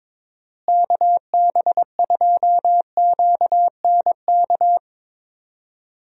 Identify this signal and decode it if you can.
Morse code, heard as a keyed tone, about 21 wpm, 700 Hz. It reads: KB2QNK
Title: USA Call Signs - 6 Characters - Repeated 3x - 22wpm